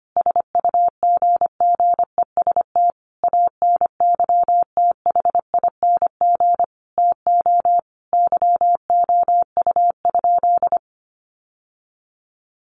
Morse code, heard as a keyed tone, about 25 wpm, 700 Hz. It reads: SUGGEST ANYTHING TO YOU?
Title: Sentences from Top 1300 Words 25wpm